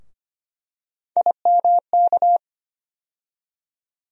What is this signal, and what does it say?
Morse code, heard as a keyed tone, about 25 wpm, 700 Hz. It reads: IMK
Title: USA Call Sign Postfixes - 3 Letters 25wpm